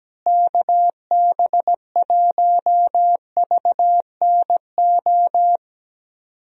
Morse code, heard as a keyed tone, about 17 wpm, 700 Hz. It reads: KB1VNO
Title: USA Call Signs - 6 Characters - Repeated 3x - 17wpm